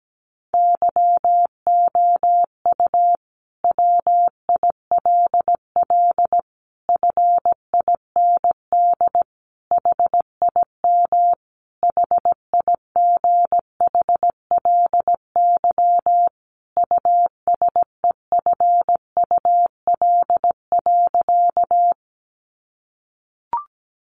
Morse code, heard as a keyed tone, about 17 wpm, 700 Hz. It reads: YOU WILL FIND HIM HIGHLY USEFUL.
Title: Sentences from Top 1100 Words 17wpm